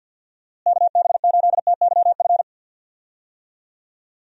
Morse code, heard as a keyed tone, about 51 wpm, 700 Hz. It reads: KB8TYF
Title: USA Call Signs - 6 Characters - Repeated 2x - 50wpm